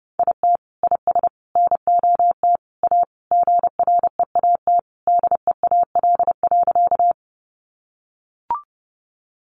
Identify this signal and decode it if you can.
Morse code, heard as a keyed tone, about 30 wpm, 700 Hz. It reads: IT IS NOT A GREAT DEAL.
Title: Sentences from Top 800 Words 30wpm